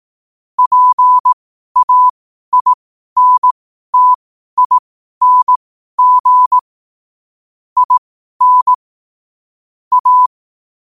Morse code, heard as a keyed tone, about 18 wpm, spaced as for 12 wpm, 1000 Hz. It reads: PAINTING IN A
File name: qotdfiles / 12wpm